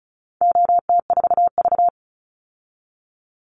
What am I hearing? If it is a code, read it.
Morse code, heard as a keyed tone, about 34 wpm, 700 Hz. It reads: OT4V
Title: Call Signs Repeated 3x - 35wpm